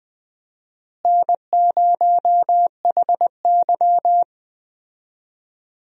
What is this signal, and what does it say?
Morse code, heard as a keyed tone, about 20 wpm, 700 Hz. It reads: N0HY
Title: USA Call Signs - 4 Characters - Repeated 3x - 20wpm